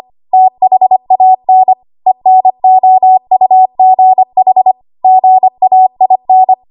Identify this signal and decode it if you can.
Morse code, heard as a keyed tone, about 25 wpm, 750 Hz. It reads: THAN ENOUGH GAIN
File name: Archive / 25 WPM / 25WPM